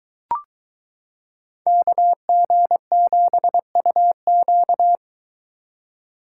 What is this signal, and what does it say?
Morse code, heard as a keyed tone, about 22 wpm, 700 Hz. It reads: KG7UQ